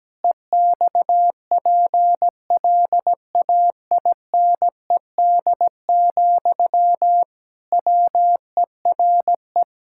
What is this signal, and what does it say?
Morse code, heard as a keyed tone, about 17 wpm, 700 Hz. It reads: EXPLAINED, WERE
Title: Pangram Challenge v2 - 17wpm